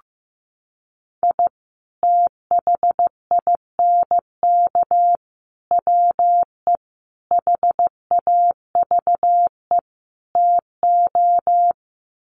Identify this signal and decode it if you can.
Morse code, heard as a keyed tone, about 15 wpm, 700 Hz. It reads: I THINK WE HAVE TO
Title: Sets of 5 Words 15wpm